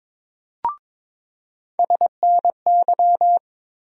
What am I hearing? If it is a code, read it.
Morse code, heard as a keyed tone, about 22 wpm, 700 Hz. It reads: SNY
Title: USA Call Sign Postfixes - 3 Letters 22wpm